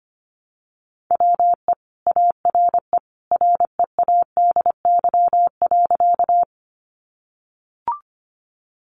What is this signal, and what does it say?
Morse code, heard as a keyed tone, about 25 wpm, 700 Hz. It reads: WE ARE READY.
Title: Sentences from Top 400 Words 25wpm